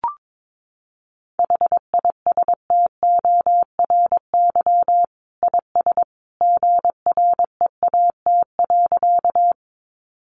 Morse code, heard as a keyed tone, about 22 wpm, 700 Hz. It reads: HISTORY IS GREAT.
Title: Sentences from Top 600 Words 22wpm